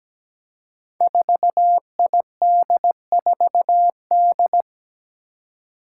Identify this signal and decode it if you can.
Morse code, heard as a keyed tone, about 17 wpm, 700 Hz. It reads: 4ID4D
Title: Five Challenging Letter-Number Combinations 17wpm